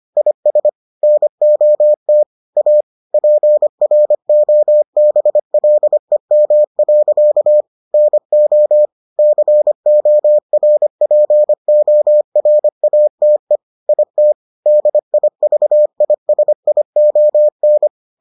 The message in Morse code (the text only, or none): IS NOT A PROBLEM. NO CORPORATE IT DIVISION